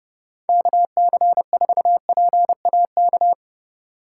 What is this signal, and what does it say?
Morse code, heard as a keyed tone, about 30 wpm, 700 Hz. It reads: KC4PAK
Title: USA Call Signs - 6 Characters - 4x Character Spacing - 30wpm